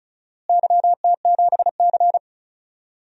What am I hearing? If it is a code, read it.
Morse code, heard as a keyed tone, about 36 wpm, 700 Hz. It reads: YT7C
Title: DXCC-100 Call Signs - 4 Characters - Repeated 3x - 35wpm